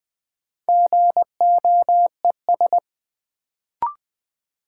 Morse code, heard as a keyed tone, about 21 wpm, 700 Hz. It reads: GOES
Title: Sentences from Top 1000 Words - Review Part 5 - New Word Drill - 20wpm